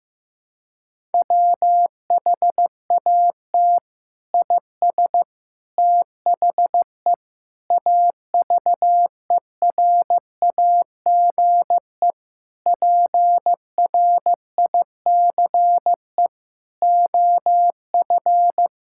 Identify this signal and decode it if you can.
Morse code, heard as a keyed tone, about 15 wpm, 700 Hz. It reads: WHAT IS THE AVERAGE PRICE OF
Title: Sentences from Top 1000 Words - Review Part 6 - 15wpm